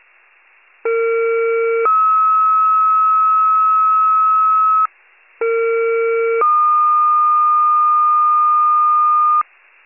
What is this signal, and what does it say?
Two-tone page, heard as a one-second tone, then a three-second tone, about 460 then 1290 Hz, about 460 then 1220 Hz. It is heard from 0.8 to 9.4 s.